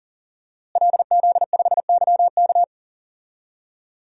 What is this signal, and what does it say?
Morse code, heard as a keyed tone, about 40 wpm, 700 Hz. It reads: LZ5YK
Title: DXCC-100 Call Signs - 5 Characters - Repeated 2x - 40wpm